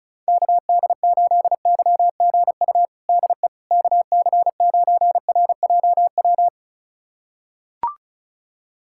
Morse code, heard as a keyed tone, about 36 wpm, 690 Hz. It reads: KD8YGU DE KC9RJW
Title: USA Callsign DE Callsign - 6 Characters - 35wpm